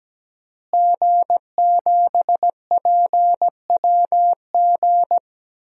Morse code, heard as a keyed tone, about 17 wpm, 700 Hz. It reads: G7PWG
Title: DXCC-100 Call Signs - 5 Characters - Repeated 3x - 17wpm